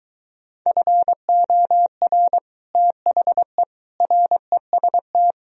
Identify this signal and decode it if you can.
Morse code, heard as a keyed tone, about 23 wpm, 700 Hz. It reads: FOR THE REST